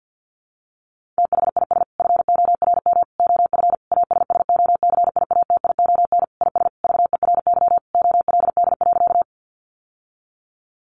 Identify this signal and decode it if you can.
Morse code, heard as a keyed tone, about 50 wpm, 700 Hz. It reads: THIS FORM OF ASSOCIATION IS VERY OLD.
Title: Sentences from Top 1100 Words 50wpm